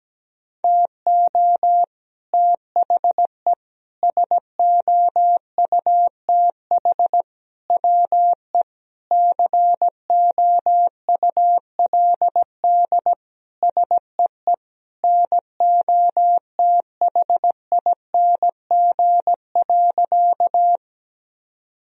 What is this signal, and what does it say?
Morse code, heard as a keyed tone, about 17 wpm, 700 Hz. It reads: TO THE SOUTH WE COULD SEE NOTHING.
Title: Sentences from Top 1000 Words - Review Part 2 - 17wpm